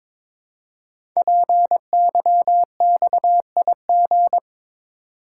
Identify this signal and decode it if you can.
Morse code, heard as a keyed tone, about 22 wpm, 700 Hz. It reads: PYXIG
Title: Code Groups - 5 Letters - 22wpm